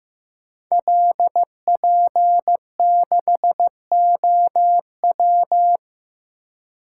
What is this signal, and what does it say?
Morse code, heard as a keyed tone, about 15 wpm, 700 Hz. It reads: LP6OW
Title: Code Groups - 5 Letter:Numbers - 15wpm